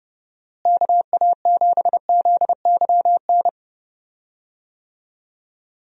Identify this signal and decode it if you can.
Morse code, heard as a keyed tone, about 30 wpm, 700 Hz. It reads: KA7ZYN
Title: USA Call Signs - 6 Characters - Repeated 2x - 30wpm